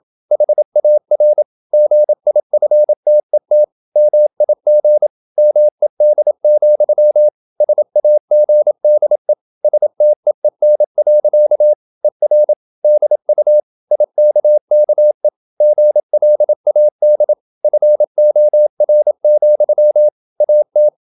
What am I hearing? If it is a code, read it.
Morse code, heard as a keyed tone, about 27 wpm, 600 Hz. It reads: HAR GIFTET MIG MED, SAGDE STEEN. ER DU IKKE GLAD FOR, AT
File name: Ferie QRQ 27wpm